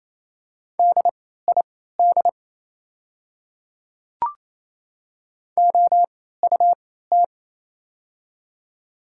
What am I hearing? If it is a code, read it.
Morse code, heard as a keyed tone, about 28 wpm, spaced as for 16 wpm, 690 Hz. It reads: DID OUT
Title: 3x Character Spacing - Top 100 Words - 28wpm